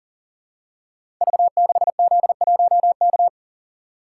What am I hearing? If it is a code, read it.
Morse code, heard as a keyed tone, about 39 wpm, 700 Hz. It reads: V6Z1K
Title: Code Groups - 5 Letter:Numbers - 40wpm